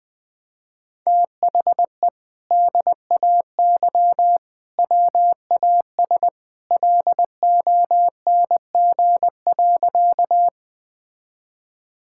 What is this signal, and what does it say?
Morse code, heard as a keyed tone, about 20 wpm, 700 Hz. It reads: THE DAY WAS LONG.
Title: Sentences with 2-Word Phrases - Spoken Beginning - 20wpm